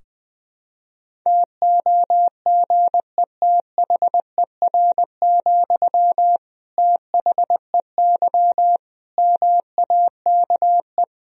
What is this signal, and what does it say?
Morse code, heard as a keyed tone, about 20 wpm, 700 Hz. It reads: TOGETHER, THEY MAKE
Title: Sentences from Top 1000 Words - Review Part 2 - 20wpm